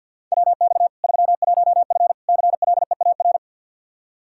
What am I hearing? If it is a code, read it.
Morse code, heard as a keyed tone, about 46 wpm, 700 Hz. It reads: WX 31F CLEAR
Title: Practice QSO - Exchange 2 - Info - 50wpm